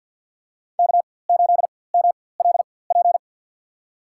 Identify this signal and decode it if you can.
Morse code, heard as a keyed tone, about 48 wpm, spaced as for 23 wpm, 700 Hz. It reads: K8MLP
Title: USA Call Signs - 5 Characters - 4x Character Spacing - 50wpm